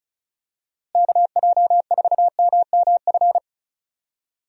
Morse code, heard as a keyed tone, about 36 wpm, 700 Hz. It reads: KJ4MMF
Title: USA Call Signs - 6 Characters - Repeated 3x - 35wpm